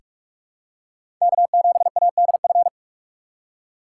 Morse code, heard as a keyed tone, about 46 wpm, 700 Hz. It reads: K7ADF
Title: USA Call Signs - 5 Characters - 3x Character Spacing - 45wpm